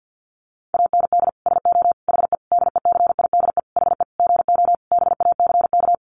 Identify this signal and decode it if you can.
Morse code, heard as a keyed tone, about 50 wpm, 700 Hz. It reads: AND SO HE DECIDE HE GO BACK
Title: Sentences from Top 500 Words 50wpm